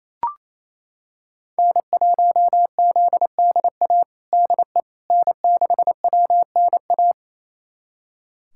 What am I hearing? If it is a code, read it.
Morse code, heard as a keyed tone, about 27 wpm, 700 Hz. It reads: N1ZDA DE N6WNA
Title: USA Callsign DE Callsign - 5 Characters - 28wpm